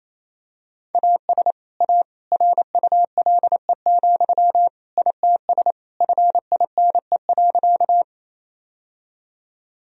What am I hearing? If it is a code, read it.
Morse code, heard as a keyed tone, about 28 wpm, 700 Hz. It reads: AS A RULE, ITS FINE.
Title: Sentences from Top 1000 Words - Review Part 2 - 28wpm